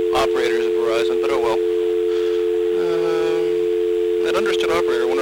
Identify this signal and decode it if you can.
Dial tone (North America), 350 + 440 Hz, with speech.
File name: blackratchet / recordings / Verizon411